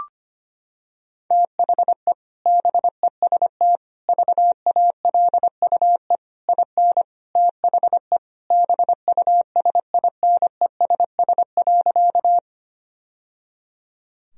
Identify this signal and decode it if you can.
Morse code, heard as a keyed tone, about 25 wpm, 700 Hz. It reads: THE BEST VALUE IN THE BUSINESS.
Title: Sentences from Top 1000 Words - Review Part 3 - 25wpm